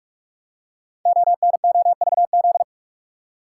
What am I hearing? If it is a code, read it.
Morse code, heard as a keyed tone, about 44 wpm, 690 Hz. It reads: ONOVZ
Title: Code Groups - 5 Letters - 45wpm